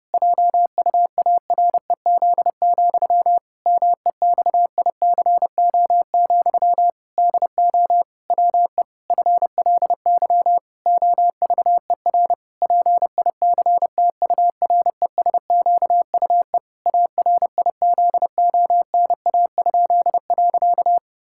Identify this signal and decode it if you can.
Morse code, heard as a keyed tone, about 32 wpm, 700 Hz. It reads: JUAREZ, MEXICO, DO WE FLY OVER PICTURESQUE ARIZONA?.
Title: Pangram Challenge v2 - 30wpm